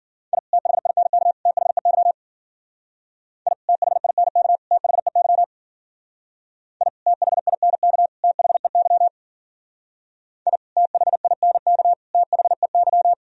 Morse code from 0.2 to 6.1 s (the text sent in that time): I THINK THEY I THINK THEY